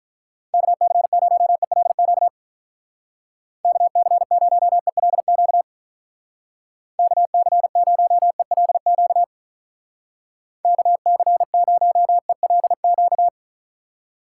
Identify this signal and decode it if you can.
Morse code, heard as a keyed tone, about 44 wpm, 700 Hz. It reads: KC0ELQ KC0ELQ KC0ELQ KC0ELQ
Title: Speed Racing - USA 2x3 Call Signs 35wpm